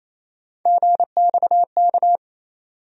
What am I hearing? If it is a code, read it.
Morse code, heard as a keyed tone, about 28 wpm, 700 Hz. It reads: GXK
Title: USA Call Sign Postfixes - 3 Letters 28wpm